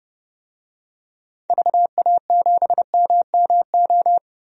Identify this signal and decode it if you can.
Morse code, heard as a keyed tone, about 31 wpm, 700 Hz. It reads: VA7MMO
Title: DXCC-100 Call Signs - 6 Characters - 30wpm